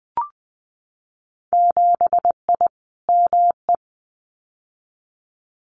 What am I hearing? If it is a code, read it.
Morse code, heard as a keyed tone, about 20 wpm, 700 Hz. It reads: 7I ME
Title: Winter Field Day Exchanges Encore 20wpm